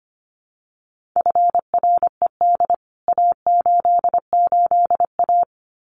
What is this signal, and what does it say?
Morse code, heard as a keyed tone, about 25 wpm, 700 Hz. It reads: FRED A88A
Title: CWT Exchange 25wpm